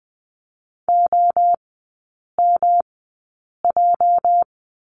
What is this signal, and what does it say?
Morse code, heard as a keyed tone, about 20 wpm, spaced as for 8 wpm, 700 Hz. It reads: OMJ
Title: 205 Lesson 33e - Letter j - Three Characters 20wpm